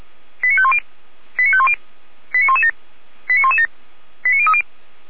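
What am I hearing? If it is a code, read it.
Five-tone selective call (ZVEI), heard as quick runs of five tones, tones of about 70 ms. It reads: 89510 89510 89117 89117 89030
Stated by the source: ZVEI five-tone selective call received on an SDR.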